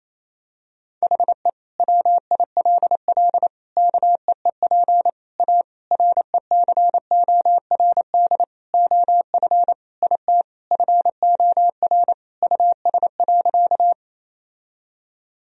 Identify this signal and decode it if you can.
Morse code, heard as a keyed tone, about 28 wpm, 700 Hz. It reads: HE WILL KEEP A RECORD OF IT FOR US.
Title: Sentences from Top 1000 Words - Review Part 3 - 28wpm